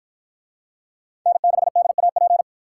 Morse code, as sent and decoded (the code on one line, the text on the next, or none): -. -.... -.. .- .--.
N6DAP